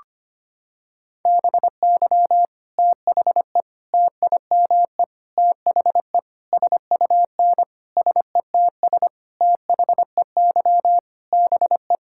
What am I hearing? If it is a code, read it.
Morse code, heard as a keyed tone, about 25 wpm, 700 Hz. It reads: BY THE TIME THE SUN SETS THEY BE
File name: practice / 25wpm / Sentences with 4-Word Phrases - 25wpm